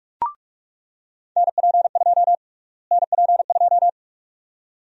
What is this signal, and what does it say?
Morse code, heard as a keyed tone, about 46 wpm, 700 Hz. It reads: NP2 NP2
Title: USA Call Sign Prefixes - Repeated 2x - 45wpm